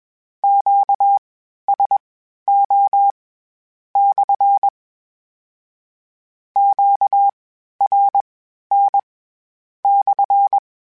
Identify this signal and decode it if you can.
Morse code, heard as a keyed tone, about 21 wpm, spaced as for 11 wpm, 800 Hz. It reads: QSO/ QRN/
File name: Q Calls / Q Calls-mp3s / Q Calls SPSL1 @20